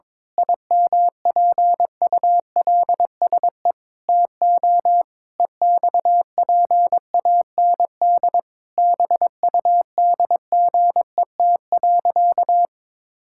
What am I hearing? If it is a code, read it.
Morse code, heard as a keyed tone, about 22 wpm, 700 Hz. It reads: IMPULSE TO EXPAND BUDGET.
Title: Pangram Challenge v2 - 22wpm